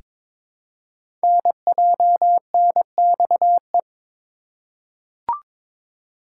Morse code, heard as a keyed tone, about 22 wpm, 700 Hz. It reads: NJNXE
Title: Code Groups - 5 Letters - 22wpm